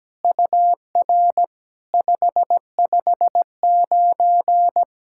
Morse code, heard as a keyed tone, about 18 wpm, 700 Hz. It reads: UR 559
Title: SOTA QSO Practice 17wpm